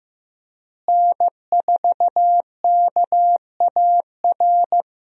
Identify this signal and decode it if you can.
Morse code, heard as a keyed tone, about 15 wpm, 700 Hz. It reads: N4KAR
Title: USA Call Signs - 5 Characters - Repeated 2x - 15wpm